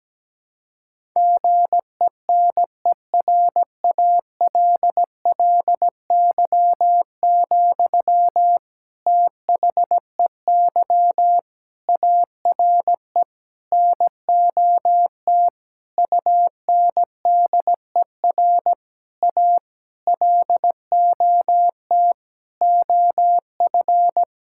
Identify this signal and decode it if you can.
Morse code, heard as a keyed tone, about 17 wpm, 700 Hz. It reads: GENERALLY, THEY ARE NOT UNDER A LOT OF
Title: Sentences from Top 1000 Words - Review Part 6 - 17wpm